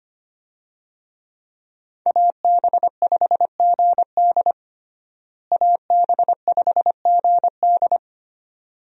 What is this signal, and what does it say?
Morse code, heard as a keyed tone, about 25 wpm, 700 Hz. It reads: AB5GD AB5GD
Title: USA Call Signs - 5 Characters - Repeated 2x - 25wpm